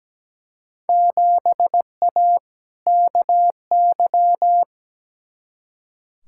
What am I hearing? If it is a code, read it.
Morse code, heard as a keyed tone, about 17 wpm, 700 Hz. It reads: 7A KY
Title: Field Day Exchanges Encore - 17wpm